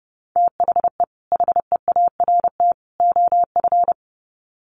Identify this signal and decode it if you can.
Morse code, heard as a keyed tone, about 30 wpm, 700 Hz. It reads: THE HEART OF
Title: Sets of 3 Words 30wpm